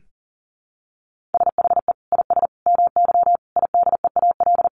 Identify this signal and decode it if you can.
Morse code, heard as a keyed tone, about 40 wpm, 700 Hz. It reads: SHE IS MY IDEAL
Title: Sentences from Top 1400 Words 40wpm